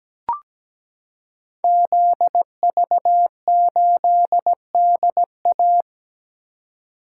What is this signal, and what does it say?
Morse code, heard as a keyed tone, about 17 wpm, 700 Hz. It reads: ZV8DA
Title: Code Groups - 5 Letter:Numbers - 17wpm